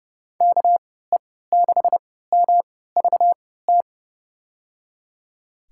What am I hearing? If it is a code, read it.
Morse code, heard as a keyed tone, about 30 wpm, spaced as for 17 wpm, 700 Hz. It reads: KE6MVT
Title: USA Call Signs - 3x Character Spacing - 30wpm